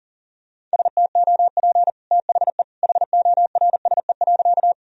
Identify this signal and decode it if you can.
Morse code, heard as a keyed tone, about 40 wpm, 700 Hz. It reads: STOP THE HORSE.
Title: Sentences from Top 1000 Words - Review Part 1 - 40wpm